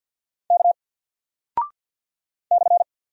Morse code, heard as a keyed tone, about 50 wpm, 700 Hz. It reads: K /
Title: Single Character (Letter:Number:Punctuation:Prosign) 50wpm